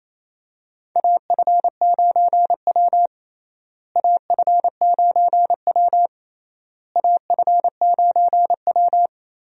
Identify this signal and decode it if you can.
Morse code, heard as a keyed tone, about 28 wpm, 700 Hz. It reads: AF9W AF9W AF9W
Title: USA Call Signs - 4 Characters - Repeated 3x - 28wpm